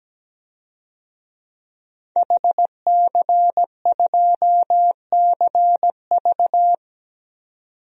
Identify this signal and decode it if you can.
Morse code, heard as a keyed tone, about 17 wpm, 700 Hz. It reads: HC2CV
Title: DXCC-100 Call Signs - 5 Characters - 17wpm